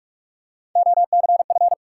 Morse code, heard as a keyed tone, about 48 wpm, 700 Hz. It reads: OCF
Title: Top 180 QSO Elements 45wpm